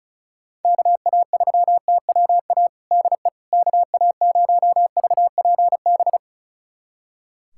Morse code, heard as a keyed tone, about 36 wpm, 700 Hz. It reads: KA3TWA DE KA0VPB
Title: USA Callsign DE Callsign - 6 Characters - 35wpm